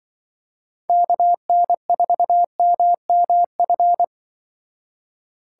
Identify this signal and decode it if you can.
Morse code, heard as a keyed tone, about 23 wpm, 700 Hz. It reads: KN4MMF
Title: Speed Racing - USA 2x3 Call Signs 20wpm